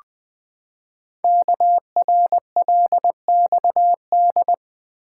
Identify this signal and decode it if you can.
Morse code, heard as a keyed tone, about 20 wpm, 700 Hz. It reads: KRLXD
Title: Code Groups - 5 Letters - 20wpm